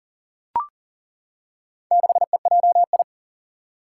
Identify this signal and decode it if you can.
Morse code, heard as a keyed tone, about 40 wpm, 700 Hz. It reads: 6EJI
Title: Four Challenging Letter-Number Combinations 40wpm